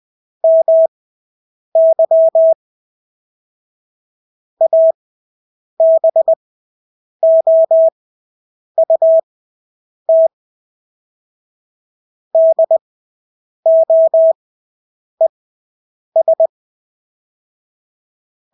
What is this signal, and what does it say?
Morse code, heard as a keyed tone, about 20 wpm, spaced as for 8 wpm, 650 Hz. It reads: MY ABOUT DOES
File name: Engelske ord 20/8 wpm